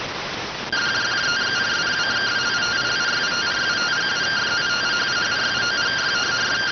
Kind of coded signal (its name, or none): two-tone teleprinter signal